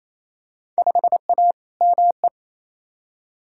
Morse code, heard as a keyed tone, about 28 wpm, 700 Hz. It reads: 5A ME
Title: Field Day Exchanges Encore - 28wpm